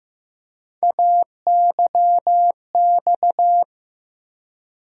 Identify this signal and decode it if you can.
Morse code, heard as a keyed tone, about 15 wpm, 700 Hz. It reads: AYX
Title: USA Call Sign Postfixes - 3 Letters 15wpm